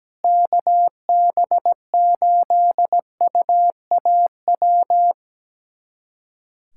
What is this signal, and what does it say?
Morse code, heard as a keyed tone, about 17 wpm, 700 Hz. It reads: KB8UAW
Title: USA Callsign DE Callsign 17wpm